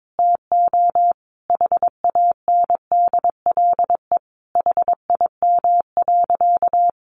Morse code, heard as a keyed tone, about 22 wpm, 700 Hz. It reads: TO HANDLE HIM.
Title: Sentences from Top 1100 Words 22wpm